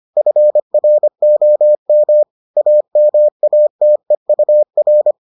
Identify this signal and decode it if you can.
Morse code, heard as a keyed tone, about 25 wpm, 600 Hz. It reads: FROM AMATEUR